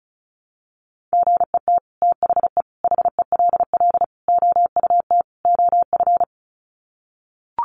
Morse code, heard as a keyed tone, about 34 wpm, 700 Hz. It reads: GET THE HELL OUT OF
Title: Sets of 5 Words Encore2 35wpm